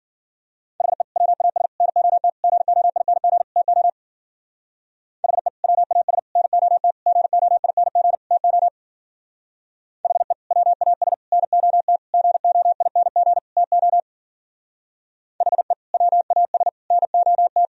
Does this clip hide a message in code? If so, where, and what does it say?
Yes: Morse code from 0.8 to 17.6 s, heard as a keyed tone, about 52 wpm, 700 Hz. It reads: HE WAS NOT GOING TO HE WAS NOT GOING TO HE WAS NOT GOING TO HE WAS NOT